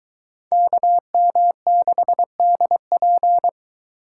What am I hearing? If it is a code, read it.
Morse code, heard as a keyed tone, about 21 wpm, 700 Hz. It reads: KM6DP